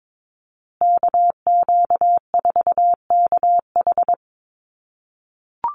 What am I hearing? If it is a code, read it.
Morse code, heard as a keyed tone, about 22 wpm, 700 Hz. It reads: KQ4KH
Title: USA Call Signs Repeated 2x 22wpm